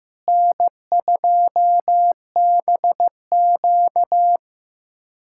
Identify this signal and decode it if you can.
Morse code, heard as a keyed tone, about 16 wpm, 700 Hz. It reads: N2BQ
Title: USA Callsign DE Callsign - 4 Characters - 15wpm